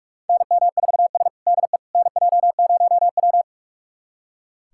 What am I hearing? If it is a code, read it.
Morse code, heard as a keyed tone, about 45 wpm, 700 Hz. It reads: NM4S DE NJ0W
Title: USA Callsign DE Callsign - 4 Characters - 45wpm